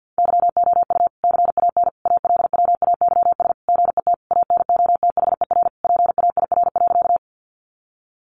Morse code, heard as a keyed tone, about 50 wpm, 700 Hz. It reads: YOU CAN ALWAYS GET ANOTHER PAIR.
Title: Sentences from Top 1200 Words 50wpm